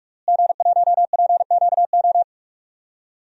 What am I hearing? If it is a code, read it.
Morse code, heard as a keyed tone, about 41 wpm, 700 Hz. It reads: G1PQO
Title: DXCC-100 Call Signs - 5 Characters - Repeated 3x - 45wpm